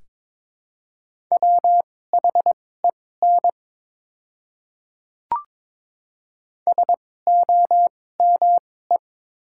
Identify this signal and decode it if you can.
Morse code, heard as a keyed tone, about 21 wpm, spaced as for 16 wpm, 690 Hz. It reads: WHEN SOME
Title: 2x Character Spacing - Top 100 Words - 22wpm